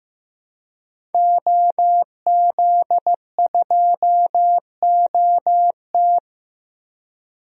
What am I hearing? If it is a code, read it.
Morse code, heard as a keyed tone, about 15 wpm, 700 Hz. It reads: OZ2OT
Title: DXCC-100 Call Signs - 5 Characters - 15wpm